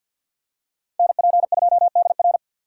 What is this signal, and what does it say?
Morse code, heard as a keyed tone, about 53 wpm, 700 Hz. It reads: NP2DR
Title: USA Call Signs - 5 Characters - 4x Character Spacing - 50wpm